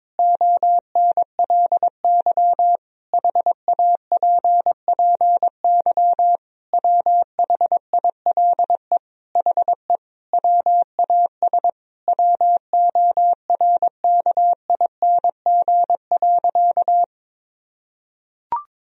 Morse code, heard as a keyed tone, about 22 wpm, 700 Hz. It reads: ONLY HAPPY WHILE HE WAS WORKING.
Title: Sentences from Top 1000 Words - Review Part 4 - 22wpm